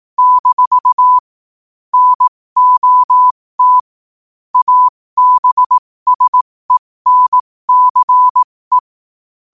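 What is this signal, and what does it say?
Morse code, heard as a keyed tone, about 18 wpm, spaced as for 15 wpm, 1000 Hz. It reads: - NOT ABSENCE
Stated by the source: Morse code quote of the day sent at 15 WPM.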